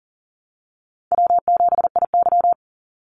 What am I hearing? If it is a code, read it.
Morse code, heard as a keyed tone, about 40 wpm, 700 Hz. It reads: W7IY
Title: Easy Call Signs Repeated 3x - 40wpm